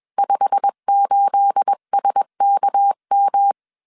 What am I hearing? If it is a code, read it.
Morse code, heard as a keyed tone, about 21 wpm, 800 Hz. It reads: 58SKM
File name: CW / DET / 1 9 1500C S B 20wpm 2313